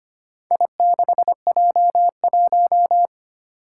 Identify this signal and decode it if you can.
Morse code, heard as a keyed tone, about 25 wpm, 700 Hz. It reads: I6J1
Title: Four Challenging Letter-Number Combinations 25wpm